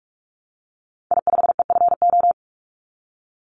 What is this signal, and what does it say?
Morse code, heard as a keyed tone, about 44 wpm, 700 Hz. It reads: I5EFO
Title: Call Signs Repeated 2x - 45wpm